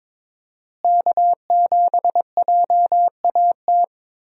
Morse code, heard as a keyed tone, about 22 wpm, 700 Hz. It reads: K7JAT
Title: USA Call Signs - 5 Characters - Repeated 2x - 22wpm